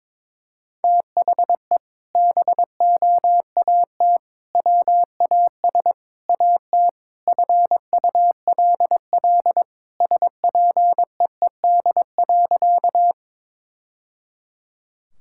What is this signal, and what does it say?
Morse code, heard as a keyed tone, about 21 wpm, 700 Hz. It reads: THE BOAT WAS AT FULL SPEED.
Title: Sentences from Top 1000 Words - Review Part 5 - 22wpm